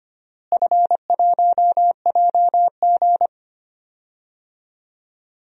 Morse code, heard as a keyed tone, about 25 wpm, 700 Hz. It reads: F1JG
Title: DXCC-100 Call Signs - 4 Characters - Repeated 3x - 25wpm